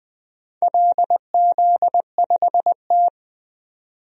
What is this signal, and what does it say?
Morse code, heard as a keyed tone, about 20 wpm, 700 Hz. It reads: LZ5T
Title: DXCC-100 Call Signs - 4 Characters - 20wpm